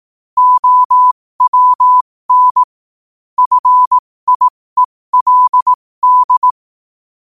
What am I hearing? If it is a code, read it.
Morse code, heard as a keyed tone, about 18 wpm, spaced as for 15 wpm, 1000 Hz. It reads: OWN FIELD
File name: 15wpm